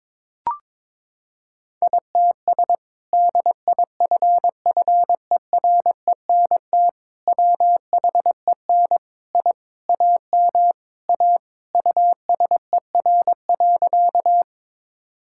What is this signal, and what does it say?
Morse code, heard as a keyed tone, about 22 wpm, 700 Hz. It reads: ITS DIFFERENT WHEN I AM A USER.
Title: Sentences from Top 1000 Words - Review Part 5 - 22wpm